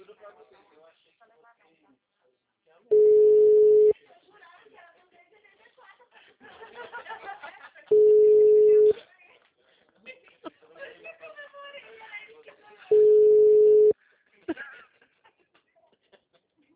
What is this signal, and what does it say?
Ringback tone (Europe, 425 Hz), with speech.